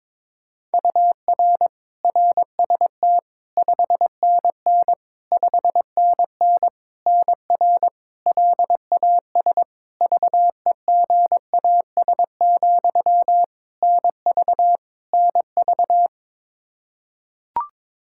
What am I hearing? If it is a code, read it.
Morse code, heard as a keyed tone, about 22 wpm, 700 Hz. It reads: UR RST 5NN 5NN NR LAS VEGAS, NV NV
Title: K1USN SST QSO Practice 22wpm